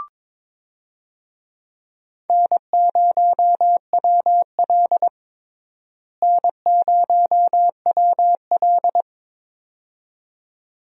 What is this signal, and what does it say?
Morse code, heard as a keyed tone, about 22 wpm, 700 Hz. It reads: N0WL N0WL
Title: USA Call Signs - 4 Characters - Repeated 2x - 22wpm